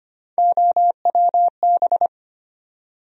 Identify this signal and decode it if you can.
Morse code, heard as a keyed tone, about 25 wpm, 700 Hz. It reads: OWB
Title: USA Call Sign Postfixes - 3 Letters 25wpm